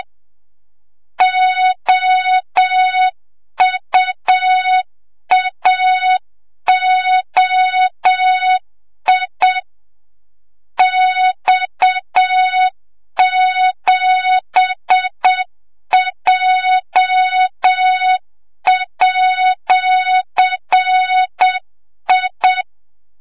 Morse code, heard as a keyed tone, about 7 wpm, 3700 Hz, with noise. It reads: OUAOI X7J@I
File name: Morze-HA2MN / hang / 25 GYAKORLAT